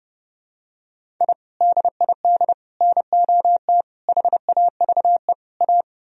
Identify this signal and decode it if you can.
Morse code, heard as a keyed tone, about 30 wpm, 700 Hz. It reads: I DID NOT HAVE A